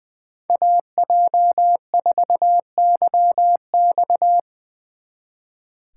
Morse code, heard as a keyed tone, about 24 wpm, 700 Hz. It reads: AJ4YX
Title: Speed Racing - USA 2x2 Call Signs 20wpm